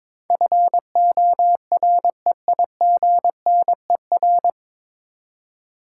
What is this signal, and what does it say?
Morse code, heard as a keyed tone, about 22 wpm, 700 Hz. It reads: FOREIGNER
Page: Morse Code Ninja practice 22wpm